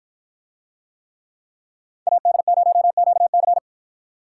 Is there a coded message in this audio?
Yes: Morse code from 2.1 to 3.6 s, heard as a keyed tone, about 52 wpm, 700 Hz. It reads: AD0QC